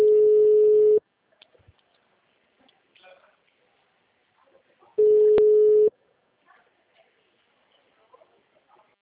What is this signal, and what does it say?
Ringback tone (Europe, 425 Hz).